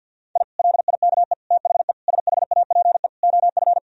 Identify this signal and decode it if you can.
Morse code, heard as a keyed tone, about 52 wpm, 700 Hz. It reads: I LIKE THE SHAPE OF
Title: Sentences from Top 1000 Words - Review Part 3 - 50wpm